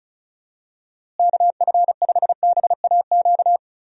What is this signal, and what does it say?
Morse code, heard as a keyed tone, about 34 wpm, 700 Hz. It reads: KF5BAQ
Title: USA Call Signs - 6 Characters - Repeated 2x - 35wpm